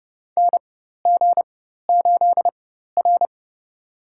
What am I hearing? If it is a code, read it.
Morse code, heard as a keyed tone, about 32 wpm, spaced as for 14 wpm, 700 Hz. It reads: NG8R
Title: USA Call Signs - 4x Character Spacing - 30wpm